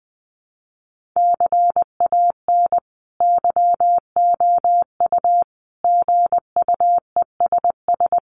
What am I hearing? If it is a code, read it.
Morse code, heard as a keyed tone, about 20 wpm, 700 Hz. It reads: CAN YOU GUESS
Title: Sentences from Top 1500 Words 20wpm